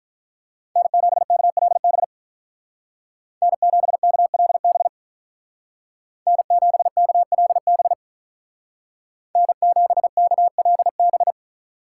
Morse code, heard as a keyed tone, about 46 wpm, 700 Hz. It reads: N7KLB N7KLB N7KLB N7KLB